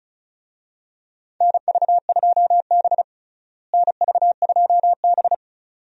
Morse code, heard as a keyed tone, about 36 wpm, 700 Hz. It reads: NV2B NV2B
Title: USA Call Signs - 4 Characters - Repeated 2x - 35wpm